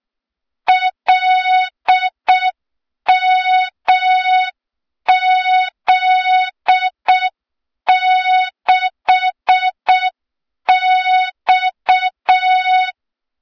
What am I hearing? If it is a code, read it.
Morse code, heard as a keyed tone, about 6 wpm, 3700 Hz. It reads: LMZ6X
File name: Morze-HA2MN / hang / 23 GYAKORLAT